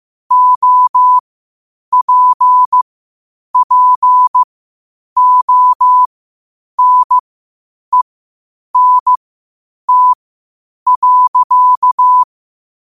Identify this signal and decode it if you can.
Morse code, heard as a keyed tone, about 15 wpm, spaced as for 8 wpm, 1000 Hz. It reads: OPPONENT.
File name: qotdfiles / 8wpm